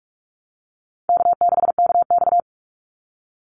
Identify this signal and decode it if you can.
Morse code, heard as a keyed tone, about 46 wpm, 700 Hz. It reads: K6KX
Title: Easy Call Signs Repeated 3x - 45wpm